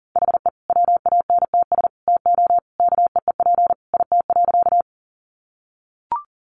Morse code, heard as a keyed tone, about 40 wpm, 700 Hz. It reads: HE WANTS TO KEEP IT.
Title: Additional Easy Phrases 40wpm